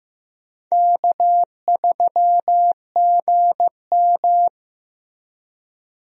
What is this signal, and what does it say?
Morse code, heard as a keyed tone, about 16 wpm, 700 Hz. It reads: K3GM
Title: USA Call Signs - 4 Characters - Repeated 3x - 15wpm